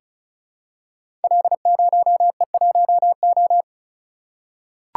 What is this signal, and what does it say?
Morse code, heard as a keyed tone, about 36 wpm, 700 Hz. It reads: L0E1O
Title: Code Groups - 5 Letter:Numbers - 35wpm